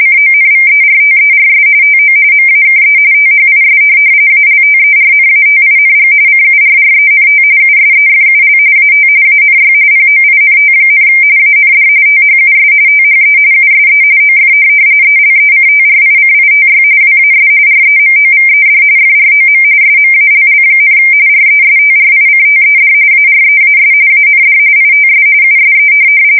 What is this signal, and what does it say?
RTTY, about 45.45 baud. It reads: CONVERTERS TO MAKE THIS ALTERNATING CURRENT USEFUL AT ANY POINT, AND THUS THE USE OF ELECTRICITY WAS REMOVED FROM SMALL, RESTRICTED AREAS AND ITS USE